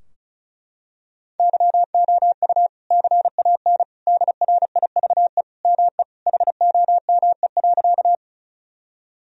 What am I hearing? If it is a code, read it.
Morse code, heard as a keyed tone, about 36 wpm, 700 Hz. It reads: YOU CAN DRIVE ME HOME.
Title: Sentences from Top 1000 Words - Review Part 2 - 35wpm